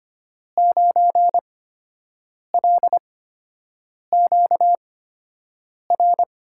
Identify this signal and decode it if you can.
Morse code, heard as a keyed tone, about 25 wpm, spaced as for 7 wpm, 700 Hz. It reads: 9LQR
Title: USA Call Signs - 5 Characters - 8x Character Spacing - 25wpm